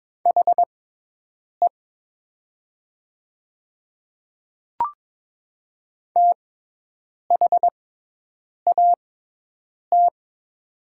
Morse code, heard as a keyed tone, about 23 wpm, spaced as for 8 wpm, 700 Hz. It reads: HE THAT
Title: 6x Character Spacing - Top 100 Words - 22wpm